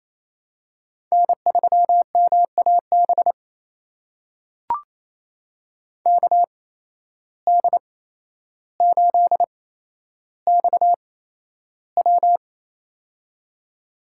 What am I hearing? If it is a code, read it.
Morse code, heard as a keyed tone, about 28 wpm, spaced as for 12 wpm, 700 Hz. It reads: N3MAB KD8XW
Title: USA Call Signs - 5 Characters - 8x Character Spacing - 28wpm